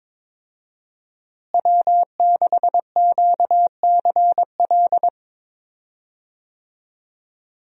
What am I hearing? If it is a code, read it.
Morse code, heard as a keyed tone, about 22 wpm, 700 Hz. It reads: W6QCL
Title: USA Call Signs - 5 Characters - Repeated 3x - 22wpm